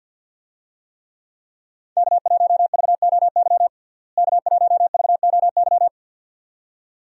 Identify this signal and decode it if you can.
Morse code, heard as a keyed tone, about 47 wpm, 700 Hz. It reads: K1VOY K1VOY
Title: USA Call Signs - 5 Characters - Repeated 2x - 50wpm